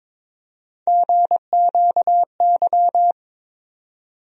Morse code, heard as a keyed tone, about 21 wpm, 700 Hz. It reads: GQY
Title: USA Call Sign Postfixes - 3 Letters 22wpm